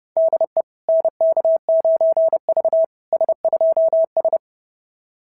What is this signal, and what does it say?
Morse code, heard as a keyed tone, about 31 wpm, 690 Hz. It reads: DE NK9V S2S
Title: SOTA QSO Practice 30wpm